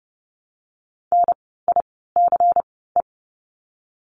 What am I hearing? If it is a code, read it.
Morse code, heard as a keyed tone, about 30 wpm, 700 Hz. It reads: N I C E
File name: Lesson 16f - Letter c - Words 30wpm